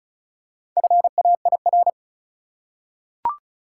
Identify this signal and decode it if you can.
Morse code, heard as a keyed tone, about 36 wpm, 700 Hz. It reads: FAIR
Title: Four Letter Words - 35wpm